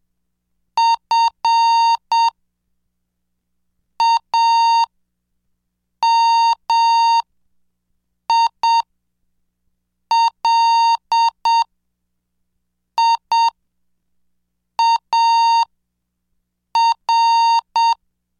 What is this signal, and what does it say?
Morse code, heard as a keyed tone, about 7 wpm, spaced as for 4 wpm, 920 Hz. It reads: FAMILIAR